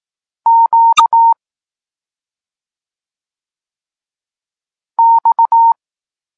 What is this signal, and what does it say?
Morse code, heard as a keyed tone, about 18 wpm, 920 Hz. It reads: Q X
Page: Tekst 32 – Doskonalenie odbioru poznanych liter